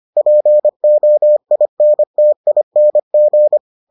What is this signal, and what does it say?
Morse code, heard as a keyed tone, about 25 wpm, 600 Hz. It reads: POINTING